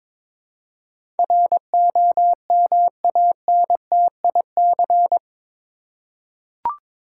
Morse code, heard as a keyed tone, about 22 wpm, 700 Hz. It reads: ROMANTIC
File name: practice / 22wpm / Eight Letter Words - 22wpm